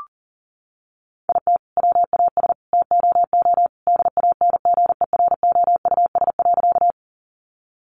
Morse code, heard as a keyed tone, about 40 wpm, 700 Hz. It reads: IT WAS TOO DANGEROUS.
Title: Sentences from Top 1200 Words 40wpm